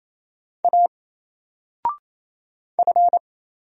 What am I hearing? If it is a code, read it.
Morse code, heard as a keyed tone, about 27 wpm, 700 Hz. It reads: A F
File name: Single-Letter-Number-28wpm